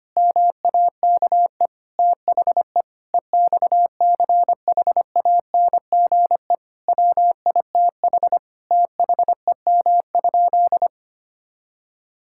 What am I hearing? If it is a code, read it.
Morse code, heard as a keyed tone, about 25 wpm, 700 Hz. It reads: MAKE THE EXCHANGE WITH THEM?
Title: Sentences from Top 1000 Words - Review Part 6 - 25wpm